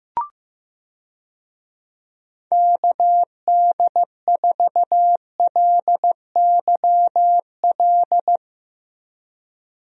Morse code, heard as a keyed tone, about 16 wpm, 700 Hz. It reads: KD4LYL
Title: USA Call Signs - 6 Characters - 15wpm